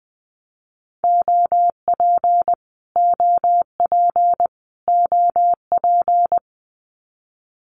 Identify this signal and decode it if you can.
Morse code, heard as a keyed tone, about 20 wpm, 700 Hz. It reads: OP OP OP
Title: Abbreviations Repeated 3x - 20wpm